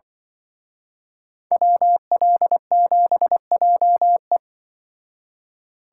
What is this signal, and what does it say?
Morse code, heard as a keyed tone, about 23 wpm, 700 Hz. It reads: WL7JE
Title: Speed Racing - USA 2x2 Call Signs 20wpm